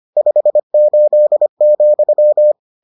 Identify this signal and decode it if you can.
Morse code, heard as a keyed tone, about 25 wpm, 600 Hz. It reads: 58,